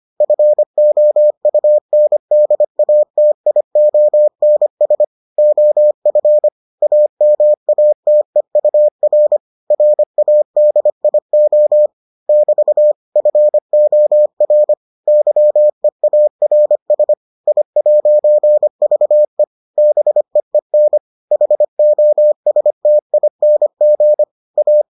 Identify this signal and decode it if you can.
Morse code, heard as a keyed tone, about 25 wpm, 600 Hz. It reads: FOUNDATIONS OF AMATEUR RADIO <BT> FOR YEARS I'VE BEEN HOSTING A